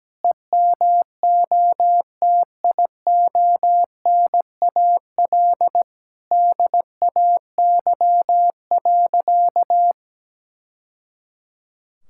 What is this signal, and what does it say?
Morse code, heard as a keyed tone, about 17 wpm, 700 Hz. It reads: EMOTIONAL DAY.
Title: Sentences from Top 1000 Words - Review Part 5 - 17wpm